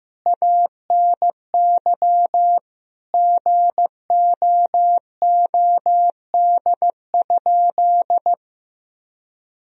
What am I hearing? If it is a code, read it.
Morse code, heard as a keyed tone, about 15 wpm, 700 Hz. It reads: ANY GOOD?
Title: Sentences from Top 1000 Words - Review Part 3 - 15wpm